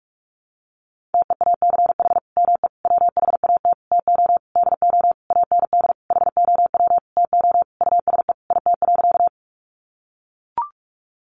Morse code, heard as a keyed tone, about 44 wpm, 700 Hz. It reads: TEACH ME WHAT TO DO AND HOW TO USE IT.
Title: Sentences from Top 500 Words 45wpm